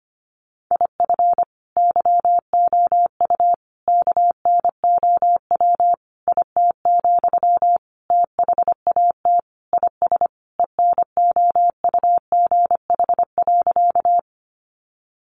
Morse code, heard as a keyed tone, about 25 wpm, 700 Hz. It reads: IF YOU KNOW IT, THAT IS ENOUGH.
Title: Sentences from Top 400 Words 25wpm